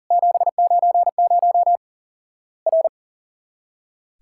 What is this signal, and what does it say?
Morse code, heard as a keyed tone, about 39 wpm, 700 Hz. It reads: 790 R
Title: Worked All Europe DX Contest - QTC Exchange - 40wpm